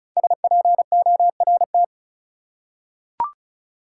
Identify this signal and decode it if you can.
Morse code, heard as a keyed tone, about 36 wpm, 700 Hz. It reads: SPORT
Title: Five Letter Words - 35wpm